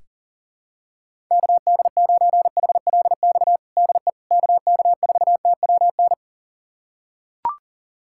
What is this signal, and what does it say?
Morse code, heard as a keyed tone, about 40 wpm, 700 Hz. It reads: KD9HLX DE KK4TWN
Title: USA Callsign DE Callsign - 6 Characters - 40wpm